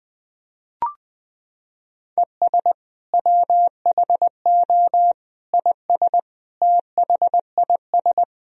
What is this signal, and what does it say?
Morse code, heard as a keyed tone, about 20 wpm, 700 Hz. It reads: ES WHO IS THIS
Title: Sentences from Top 300 Words and Abbreviations 20wpm